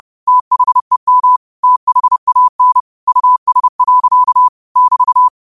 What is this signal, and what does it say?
Morse code, heard as a keyed tone, about 30 wpm, 1000 Hz. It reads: THEM THAN US. <BT>